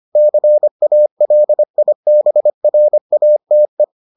Morse code, heard as a keyed tone, about 25 wpm, 600 Hz. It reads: CALIBRATE